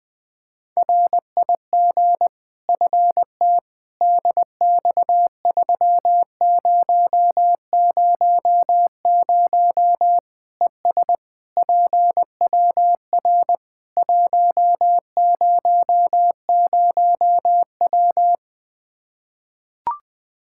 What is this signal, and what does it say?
Morse code, heard as a keyed tone, about 20 wpm, 700 Hz. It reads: RIG FT DX3000 ES PWR 100W
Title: Practice QSO - Exchange 2 - Info - 20wpm